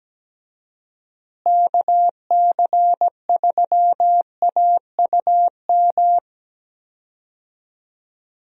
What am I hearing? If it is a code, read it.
Morse code, heard as a keyed tone, about 17 wpm, 700 Hz. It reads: KC3AUM
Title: USA Call Signs - 6 Characters - Repeated 2x - 17wpm